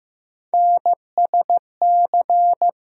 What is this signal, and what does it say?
Morse code, heard as a keyed tone, about 15 wpm, 700 Hz. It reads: NSC
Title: USA Call Sign Postfixes - 3 Letters 15wpm